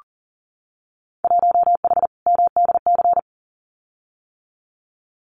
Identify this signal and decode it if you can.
Morse code, heard as a keyed tone, about 40 wpm, 700 Hz. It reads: 1H MDC
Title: Winter Field Day Exchanges 40wpm